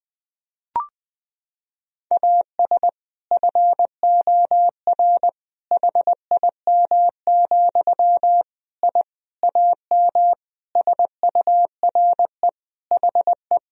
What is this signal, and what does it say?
Morse code, heard as a keyed tone, about 20 wpm, 700 Hz. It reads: AS FOR HIM, I AM SURE HE
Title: Sentences from Top 1000 Words - Review Part 2 - 20wpm